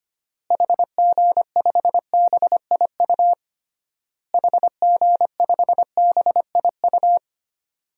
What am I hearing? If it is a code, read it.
Morse code, heard as a keyed tone, about 25 wpm, 700 Hz. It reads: HG5BIU HG5BIU
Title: DXCC-100 Call Signs - 6 Characters - Repeated 3x - 25wpm